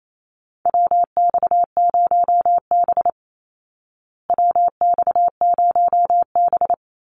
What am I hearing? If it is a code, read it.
Morse code, heard as a keyed tone, about 28 wpm, 700 Hz. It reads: WX0B WX0B
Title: Call Signs Repeated 3x - 28wpm